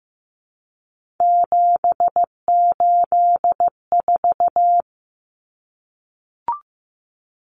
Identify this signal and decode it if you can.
Morse code, heard as a keyed tone, about 15 wpm, 700 Hz. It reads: 784
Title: Numbers - 3-Digit 15wpm